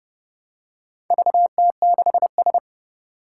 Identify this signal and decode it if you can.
Morse code, heard as a keyed tone, about 30 wpm, 700 Hz. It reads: VT6S
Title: Four Challenging Letter-Number Combinations 30wpm